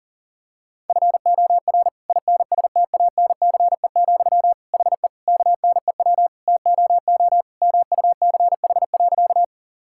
Morse code, heard as a keyed tone, about 37 wpm, 700 Hz. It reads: FOR INSTANCE, HE KNEW TOO MUCH.
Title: Sentences from Top 1000 Words - Review Part 5 - 40wpm